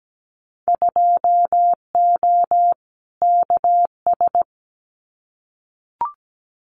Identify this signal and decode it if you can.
Morse code, heard as a keyed tone, about 17 wpm, 700 Hz. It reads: 2O KS
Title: Winter Field Day Exchanges Encore 17wpm